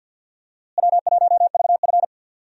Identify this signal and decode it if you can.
Morse code, heard as a keyed tone, about 46 wpm, 700 Hz. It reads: W1VF
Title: USA Call Signs - 4 Characters - Repeated 3x - 50wpm